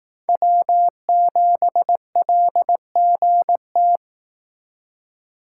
Morse code, heard as a keyed tone, about 18 wpm, 700 Hz. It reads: W7LGT
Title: Speed Racing - USA 1x3 Call Signs 15wpm